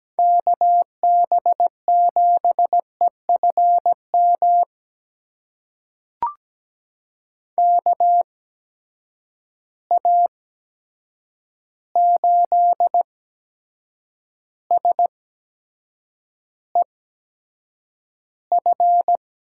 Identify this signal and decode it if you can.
Morse code, heard as a keyed tone, about 17 wpm, spaced as for 7 wpm, 700 Hz. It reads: KB7EFM KA8SEF
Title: USA Call Signs - 6 Characters - 8x Character Spacing - 17wpm